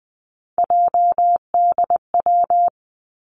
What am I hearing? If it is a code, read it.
Morse code, heard as a keyed tone, about 20 wpm, 700 Hz. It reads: JDW
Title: Three Challenging Letter-Number Combinations 20wpm